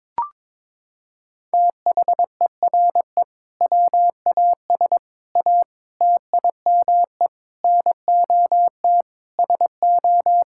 Morse code, heard as a keyed tone, about 22 wpm, 700 Hz. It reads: THERE WAS A TIME NOT SO
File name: practice / 22wpm / Sets of 8 Words - 22wpm